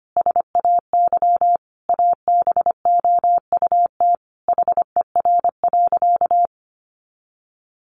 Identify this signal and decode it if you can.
Morse code, heard as a keyed tone, about 25 wpm, 700 Hz. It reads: SAY ABOUT HER.
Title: Sentences from Top 200 Words 25wpm